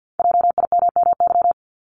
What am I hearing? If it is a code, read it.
Morse code, heard as a keyed tone, about 50 wpm, 700 Hz. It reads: JIMMY
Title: Multi-letter Phonemes Repeated 3x 50wpm